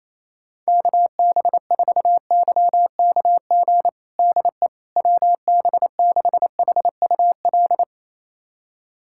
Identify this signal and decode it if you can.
Morse code, heard as a keyed tone, about 28 wpm, 700 Hz. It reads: KB4YKG DE WB6HUL
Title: USA Callsign DE Callsign 28wpm